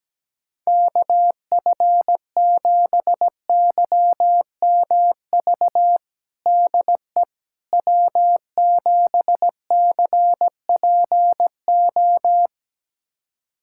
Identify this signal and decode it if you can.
Morse code, heard as a keyed tone, about 17 wpm, 700 Hz. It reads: KF7YMV DE W7CPO
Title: USA Callsign DE Callsign 17wpm